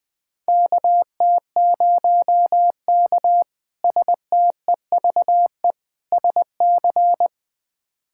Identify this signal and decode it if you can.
Morse code, heard as a keyed tone, about 20 wpm, 700 Hz. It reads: KT0K STEVE SC
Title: K1USN SST QSO Practice - 20wpm